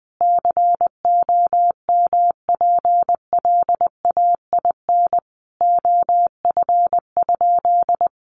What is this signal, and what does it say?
Morse code, heard as a keyed tone, about 20 wpm, 700 Hz. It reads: COMPLAIN OF?
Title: Sentences from Top 1300 Words 20wpm